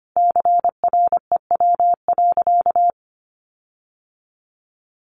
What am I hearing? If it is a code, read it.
Morse code, heard as a keyed tone, about 25 wpm, 700 Hz. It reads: CREW.
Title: Sentences from Top 1500 Words 25wpm